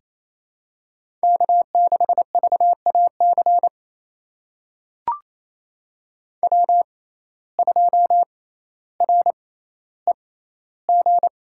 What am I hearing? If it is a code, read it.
Morse code, heard as a keyed tone, about 28 wpm, spaced as for 14 wpm, 700 Hz. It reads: K6VAC W2REG
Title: USA Call Signs - 5 Characters - 6x Character Spacing - 28wpm